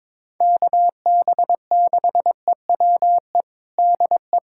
Morse code, heard as a keyed tone, about 22 wpm, 700 Hz. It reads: KB6EWE DE
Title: USA Callsign DE Callsign - 6 Characters - 22wpm